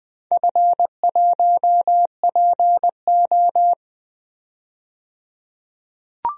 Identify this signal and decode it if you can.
Morse code, heard as a keyed tone, about 20 wpm, 700 Hz. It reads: F1PO
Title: DXCC-100 Call Signs - 4 Characters - 20wpm